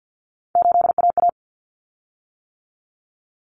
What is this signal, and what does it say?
Morse code, heard as a keyed tone, about 50 wpm, 700 Hz. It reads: 8AA
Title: Common Cut Numbers - 3-Digit 50wpm